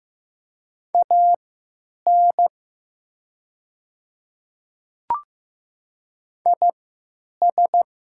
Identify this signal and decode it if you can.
Morse code, heard as a keyed tone, about 15 wpm, spaced as for 9 wpm, 700 Hz. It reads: AN IS
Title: 3x Character Spacing - Top 100 Words - 15wpm